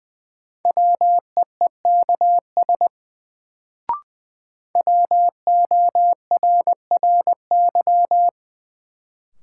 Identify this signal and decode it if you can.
Morse code, heard as a keyed tone, about 20 wpm, 700 Hz. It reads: WEEKS WORRY
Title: Sets of 5 Words - New Word Drill - 20wpm